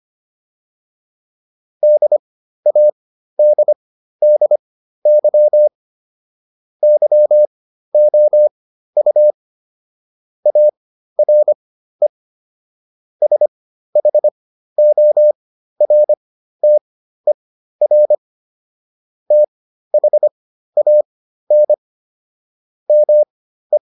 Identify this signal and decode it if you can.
Morse code, heard as a keyed tone, about 25 wpm, spaced as for 13 wpm, 600 Hz. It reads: DADDY YOU ARE SHORTER THAN ME